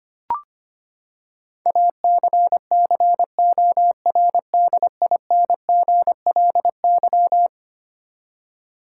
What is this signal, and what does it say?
Morse code, heard as a keyed tone, about 25 wpm, 700 Hz. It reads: ACCORDINGLY
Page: morse-code-ninja-practice-25wpm